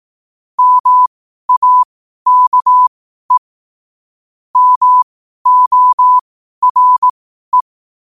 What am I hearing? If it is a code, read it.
Morse code, heard as a keyed tone, about 18 wpm, spaced as for 12 wpm, 1000 Hz. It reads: MAKE MORE
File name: qotdfiles / 12wpm